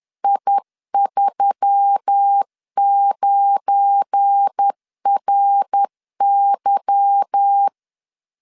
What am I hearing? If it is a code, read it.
Morse code, heard as a keyed tone, about 10 wpm, 800 Hz. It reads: I39RY
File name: LEZIONI / 8 YJ45 / 8 Y 10wpm 2615